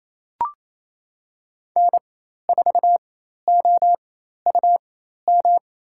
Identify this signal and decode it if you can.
Morse code, heard as a keyed tone, about 28 wpm, spaced as for 13 wpm, 700 Hz. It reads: N4OUM
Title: USA Call Signs - 5 Characters - 4x Character Spacing - 28wpm